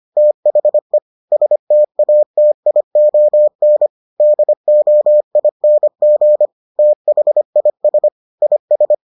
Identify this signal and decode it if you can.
Morse code, heard as a keyed tone, about 25 wpm, 600 Hz. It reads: THE STATION DOING THIS IS